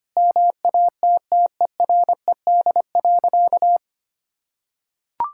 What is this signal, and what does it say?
Morse code, heard as a keyed tone, about 25 wpm, 700 Hz. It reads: MATTERED.
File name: practice / 25wpm / Sentences with 4-Word Phrases - 25wpm